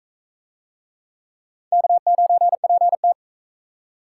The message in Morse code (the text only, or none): K9PT